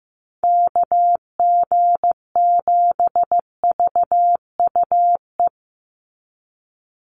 Morse code, heard as a keyed tone, about 15 wpm, 700 Hz. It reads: KG7VUE
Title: USA Call Signs 15wpm